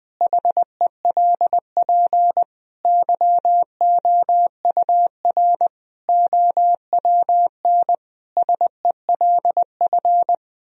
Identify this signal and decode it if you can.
Morse code, heard as a keyed tone, about 20 wpm, 700 Hz. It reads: HELP YOUR OWN SELF
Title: Sentences from Top 1000 Words - Review Part 1 - 20wpm